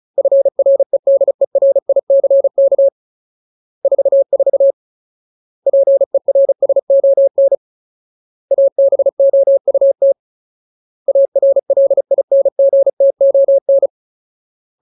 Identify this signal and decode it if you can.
Morse code, heard as a keyed tone, about 34 wpm, 550 Hz. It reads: FREDERICK 44 PERSON ABOUT ARLINGTON
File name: Ultimate Word List / 35WPM Ultimate